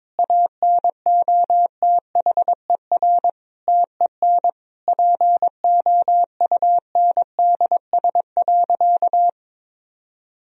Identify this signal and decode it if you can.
Morse code, heard as a keyed tone, about 22 wpm, 700 Hz. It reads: ANOTHER TEN POUNDS.
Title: Sentences from Top 1000 Words - Review Part 4 - 22wpm